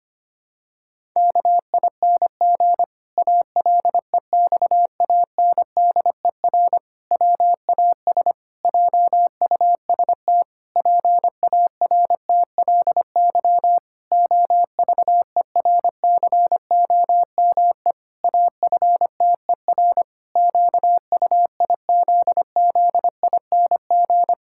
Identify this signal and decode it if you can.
Morse code, heard as a keyed tone, about 25 wpm, 700 Hz. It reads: KING ALEXANDER WAS JUST PARTLY OVERCOME AFTER QUIZZING
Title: Pangram Challenge v2 - 25wpm